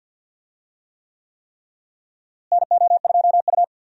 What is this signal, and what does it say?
Morse code, heard as a keyed tone, about 51 wpm, 700 Hz. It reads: NO2U
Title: USA Call Signs - 4 Characters - Repeated 3x - 50wpm